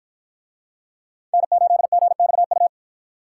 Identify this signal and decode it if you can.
Morse code, heard as a keyed tone, about 48 wpm, 700 Hz. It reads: N8GXU